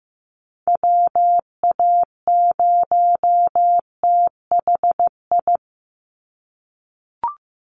Morse code, heard as a keyed tone, about 15 wpm, 700 Hz. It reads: WA0THI
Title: USA Call Signs 15wpm